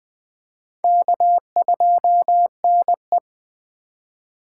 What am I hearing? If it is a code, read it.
Morse code, heard as a keyed tone, about 21 wpm, 700 Hz. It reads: K2NE
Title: Speed Racing - USA 1x2 Call Signs 17wpm